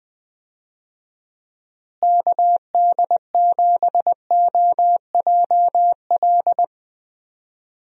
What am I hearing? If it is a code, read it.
Morse code, heard as a keyed tone, about 20 wpm, 700 Hz. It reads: KD7OJL
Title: USA Call Signs - 6 Characters - Repeated 2x - 20wpm